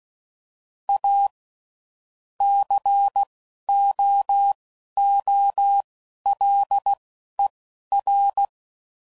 Morse code, heard as a keyed tone, about 16 wpm, spaced as for 11 wpm, 800 Hz. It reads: A COOLER
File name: CW Code Groups @10WPM / Weather Sets / Weather 5 5